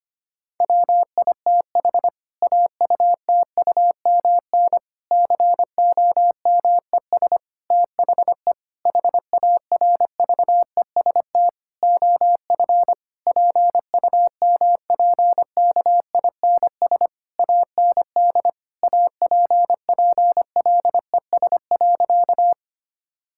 Morse code, heard as a keyed tone, about 25 wpm, 700 Hz. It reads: WITH AUTUMN COMES THE HARVEST OF PUMPKINS AND APPLES.
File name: practice / 25wpm / Difficult Sentences with Hints - 25wpm